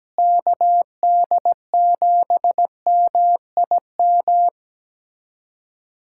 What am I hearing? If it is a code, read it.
Morse code, heard as a keyed tone, about 17 wpm, 700 Hz. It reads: KD7MIM
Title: USA Callsign DE Callsign - 6 Characters - 17wpm